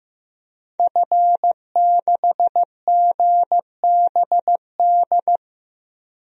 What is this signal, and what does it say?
Morse code, heard as a keyed tone, about 16 wpm, 700 Hz. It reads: F6GBD
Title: DXCC-100 Call Signs - 5 Characters - 15wpm